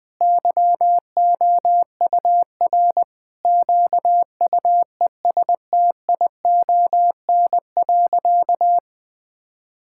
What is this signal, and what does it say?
Morse code, heard as a keyed tone, about 20 wpm, 700 Hz. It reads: YOUR QUESTION.
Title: Sentences from Top 1000 Words - Review Part 6 - 20wpm